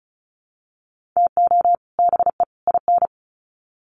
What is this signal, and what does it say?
Morse code, heard as a keyed tone, about 34 wpm, 700 Hz. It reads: TO BE IN
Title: Sets of 3 Words 35wpm